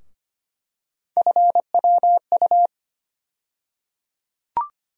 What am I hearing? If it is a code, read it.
Morse code, heard as a keyed tone, about 25 wpm, 700 Hz. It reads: FWU
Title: USA Call Sign Postfixes - 3 Letters 25wpm